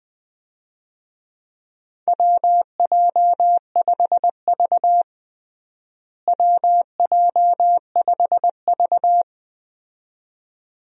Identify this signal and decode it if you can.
Morse code, heard as a keyed tone, about 20 wpm, 700 Hz. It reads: WJ5V WJ5V
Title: USA Call Signs - 4 Characters - Repeated 2x - 20wpm